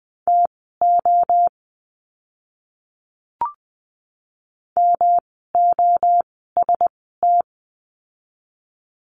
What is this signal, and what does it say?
Morse code, heard as a keyed tone, about 20 wpm, spaced as for 15 wpm, 700 Hz. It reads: TO MOST
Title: 206 Lesson 33f - Letter j - Words 20wpm